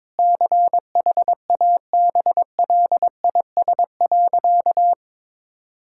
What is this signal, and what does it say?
Morse code, heard as a keyed tone, about 22 wpm, 700 Hz. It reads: CHABLIS.
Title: Pangram Challenge v2 - 22wpm